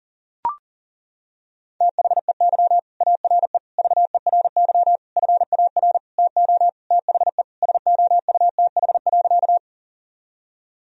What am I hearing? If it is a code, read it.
Morse code, heard as a keyed tone, about 38 wpm, 700 Hz. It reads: THEY ARE VERY FAR TO THE SOUTH.
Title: Sentences from Top 1000 Words - Review Part 2 - 40wpm